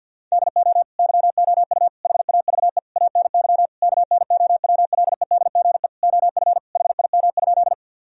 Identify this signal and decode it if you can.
Morse code, heard as a keyed tone, about 50 wpm, 700 Hz. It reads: DO YOU HAVE ANY KNOWLEDGE OF HIM?
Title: Sentences from Top 1000 Words - Review Part 3 - 50wpm